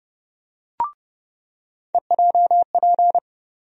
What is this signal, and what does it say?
Morse code, heard as a keyed tone, about 32 wpm, 700 Hz. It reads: EJP
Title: USA Call Sign Postfixes - 3 Letters 30wpm